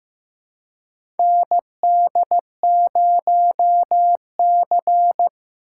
Morse code, heard as a keyed tone, about 15 wpm, 700 Hz. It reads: ND0C
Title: USA Call Signs - 4 Characters - 3x Character Spacing - 15wpm